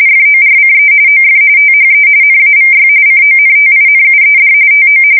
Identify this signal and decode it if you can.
RTTY, about 45.45 baud. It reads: FORCE OF THE NAVY CONTINUED TO
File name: fsk170 / usnavy2